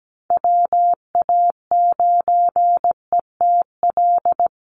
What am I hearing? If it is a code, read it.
Morse code, heard as a keyed tone, about 17 wpm, 700 Hz. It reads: WA9ETL
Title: USA Call Signs 17wpm